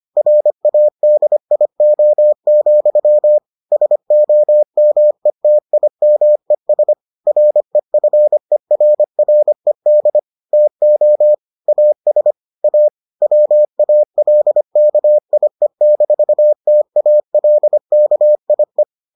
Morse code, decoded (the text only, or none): RADIO, SOMETIMES REFERRED TO AS A WALKIE-TALKIE